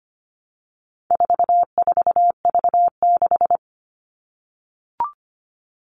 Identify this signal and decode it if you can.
Morse code, heard as a keyed tone, about 25 wpm, 700 Hz. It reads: 44V6
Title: All Cut Numbers - 4-Digit 25wpm